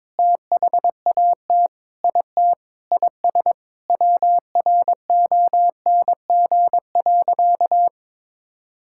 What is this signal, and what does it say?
Morse code, heard as a keyed tone, about 22 wpm, 700 Hz. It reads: THAT IT IS WRONG.
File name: Sentences with 7-Word Phrases - 22wpm